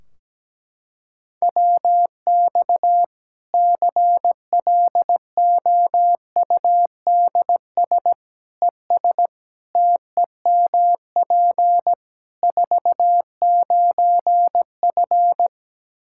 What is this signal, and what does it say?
Morse code, heard as a keyed tone, about 17 wpm, 700 Hz. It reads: WX CLOUDS ES TEMP 49F
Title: K1USN SST QSO Practice 17wpm